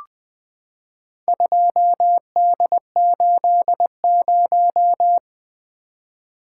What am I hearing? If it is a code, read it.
Morse code, heard as a keyed tone, about 20 wpm, 700 Hz. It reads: 2D80
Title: Four Letter-Number Combinations - 20wpm